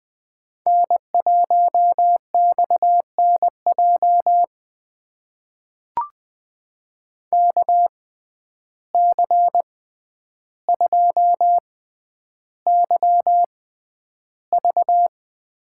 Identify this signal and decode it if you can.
Morse code, heard as a keyed tone, about 20 wpm, spaced as for 10 wpm, 700 Hz. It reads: N1XNJ KC2YV
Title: USA Call Signs - 6x Character Spacing - 20wpm